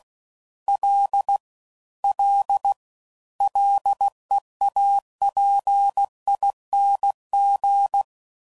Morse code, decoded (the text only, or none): L L LEAPING